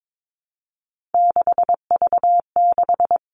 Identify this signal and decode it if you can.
Morse code, heard as a keyed tone, about 22 wpm, 700 Hz. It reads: 6V6
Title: Three Challenging Letter-Number Combinations 22wpm